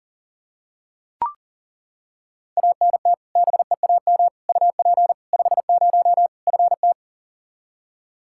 Morse code, heard as a keyed tone, about 40 wpm, 700 Hz. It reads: ANT BEAM UP 50 FT
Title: Practice QSO - Exchange 2 - Info - 40wpm